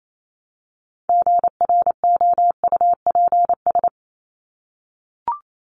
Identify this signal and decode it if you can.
Morse code, heard as a keyed tone, about 28 wpm, 700 Hz. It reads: GROUPS
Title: ES:S Suffix - Top 100 Words with Base Words 28wpm